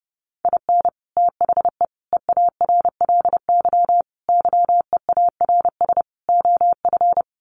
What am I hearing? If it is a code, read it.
Morse code, heard as a keyed tone, about 28 wpm, 700 Hz. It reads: IN THE EARLY YEARS OF
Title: Sets of 5 Words Encore 30wpm